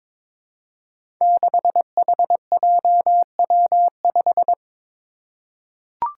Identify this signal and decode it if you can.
Morse code, heard as a keyed tone, about 22 wpm, 700 Hz. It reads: 6HJW5
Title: Five Challenging Letter-Number Combinations 22wpm